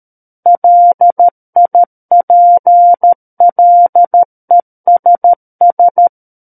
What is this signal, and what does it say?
Morse code, heard as a keyed tone, about 13 wpm, 700 Hz. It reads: LIPLESS
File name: Book One / 04-Chapter 4 - 13 WPM